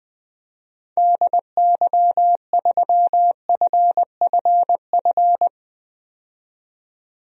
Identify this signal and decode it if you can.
Morse code, heard as a keyed tone, about 21 wpm, 700 Hz. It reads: DY3FFF
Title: DXCC-100 Call Signs - 6 Characters - 20wpm